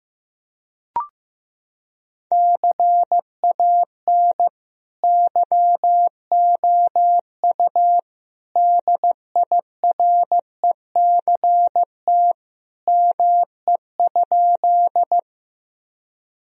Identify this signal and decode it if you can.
Morse code, heard as a keyed tone, about 15 wpm, 700 Hz. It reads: CAN YOU DIRECT ME?
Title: Sentences from Top 1000 Words - Review Part 2 - 15wpm